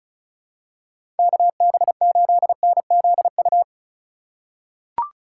Morse code, heard as a keyed tone, about 36 wpm, 700 Hz. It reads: KB8NZU
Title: USA Call Signs - 6 Characters - 3x Character Spacing - 35wpm